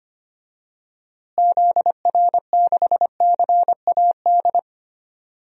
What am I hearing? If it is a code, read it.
Morse code, heard as a keyed tone, about 25 wpm, 700 Hz. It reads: ZR6CAD
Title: DXCC-100 Call Signs - 6 Characters - Repeated 2x - 25wpm